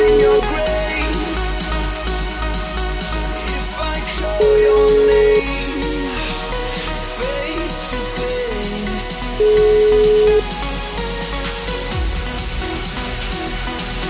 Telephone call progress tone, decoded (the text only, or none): ringback tone